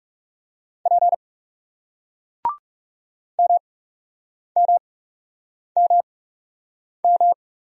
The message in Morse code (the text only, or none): P M M M M